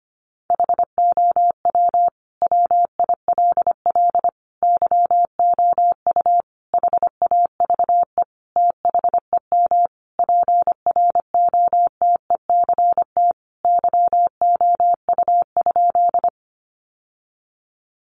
Morse code, heard as a keyed tone, about 25 wpm, 700 Hz. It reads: HOW WILL YOU HAVE THEM PROTECT YOU?
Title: Sentences from Top 900 Words 25wpm